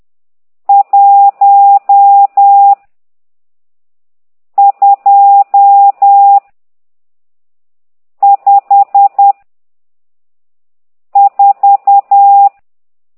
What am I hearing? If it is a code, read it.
Morse code, heard as a keyed tone, about 10 wpm, 800 Hz. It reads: 1 2 5 4
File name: ct2idw / exerc12